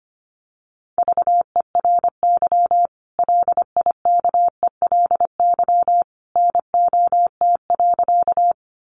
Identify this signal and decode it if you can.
Morse code, heard as a keyed tone, about 25 wpm, 700 Hz. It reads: VERY LIKELY NOT.
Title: Sentences from Top 600 Words 25wpm